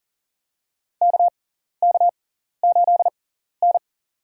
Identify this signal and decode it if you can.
Morse code, heard as a keyed tone, about 38 wpm, spaced as for 14 wpm, 700 Hz. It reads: KK8N
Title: USA Call Signs - 4 Characters - 6x Character Spacing - 40wpm